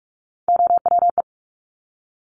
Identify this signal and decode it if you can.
Morse code, heard as a keyed tone, about 46 wpm, 700 Hz. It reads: OWE
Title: Top Three Letter Combinations in Words 45wpm